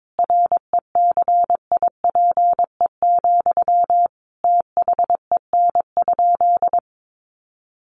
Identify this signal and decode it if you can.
Morse code, heard as a keyed tone, about 22 wpm, 700 Hz. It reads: RECIPE, THEN?
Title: Sentences from Top 1200 Words 22wpm